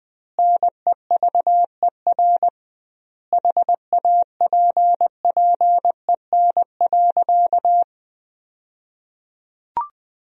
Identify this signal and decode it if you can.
Morse code, heard as a keyed tone, about 20 wpm, 700 Hz. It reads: NEVER HAPPEN.
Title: 2x Word-Spacing - Sentences from Top 400 Words - 20wpm